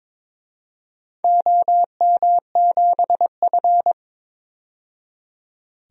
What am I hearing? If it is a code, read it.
Morse code, heard as a keyed tone, about 22 wpm, 700 Hz. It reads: OM7F
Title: DXCC-100 Call Signs - 4 Characters - 22wpm